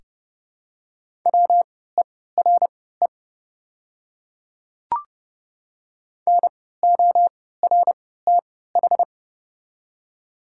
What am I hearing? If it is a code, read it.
Morse code, heard as a keyed tone, about 30 wpm, spaced as for 17 wpm, 700 Hz. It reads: WERE NORTH
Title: 3x Character Spacing - Top 300 Words - 30wpm